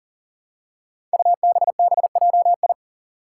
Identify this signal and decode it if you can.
Morse code, heard as a keyed tone, about 40 wpm, 700 Hz. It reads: UBBJI
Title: Five Challenging Letter-Number Combinations 40wpm